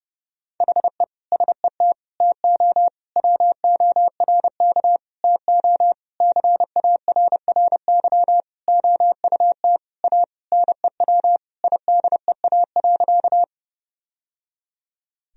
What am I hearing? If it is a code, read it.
Morse code, heard as a keyed tone, about 30 wpm, 700 Hz. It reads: HE SET TO WORK TO CARRY OUT A NEW IDEA.
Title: Sentences from Top 1000 Words - Review Part 1 - 30wpm